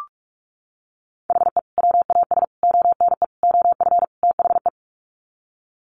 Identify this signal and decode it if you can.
Morse code, heard as a keyed tone, about 44 wpm, 700 Hz. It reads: HE WAS ONE OF THE
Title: Sets of 5 Words 45wpm